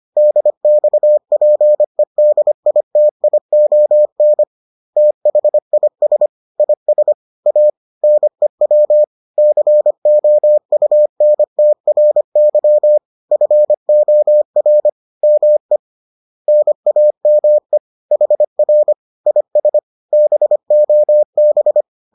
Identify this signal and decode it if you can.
Morse code, heard as a keyed tone, about 25 wpm, 600 Hz. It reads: DXPEDITION THIS IS A NEW COUNTRY FOR ME NAME HR IS BOB